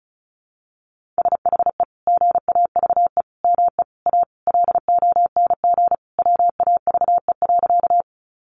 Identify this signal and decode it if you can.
Morse code, heard as a keyed tone, about 34 wpm, 700 Hz. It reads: SHE GAVE ME A LONG WAVE.
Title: Sentences from Top 1400 Words 35wpm